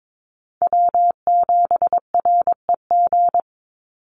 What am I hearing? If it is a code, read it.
Morse code, heard as a keyed tone, about 22 wpm, 700 Hz. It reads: W7REG
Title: USA Call Signs 22wpm